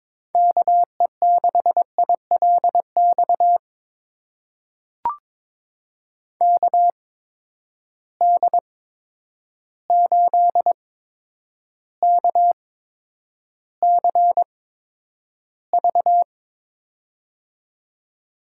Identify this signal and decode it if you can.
Morse code, heard as a keyed tone, about 22 wpm, spaced as for 9 wpm, 700 Hz. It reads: KE6ILX KD8KCV
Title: USA Call Signs - 6 Characters - 8x Character Spacing - 22wpm